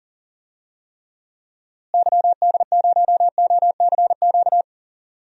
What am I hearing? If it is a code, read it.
Morse code, heard as a keyed tone, about 40 wpm, 700 Hz. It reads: YD0OCQ
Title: DXCC-100 Call Signs - 6 Characters - Repeated 3x - 40wpm